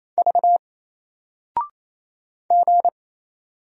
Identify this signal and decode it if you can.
Morse code, heard as a keyed tone, about 27 wpm, 700 Hz. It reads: V G
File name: Single-Letter-Number-28wpm